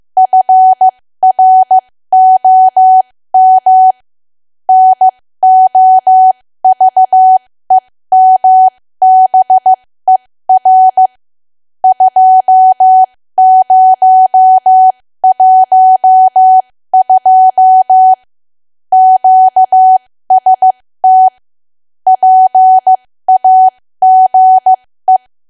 Morse code, heard as a keyed tone, about 15 wpm, spaced as for 13 wpm, 750 Hz. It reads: FROM NOVEMBER 2012 QST PAGE
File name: cwa / W1AW-Level2 / 13WPM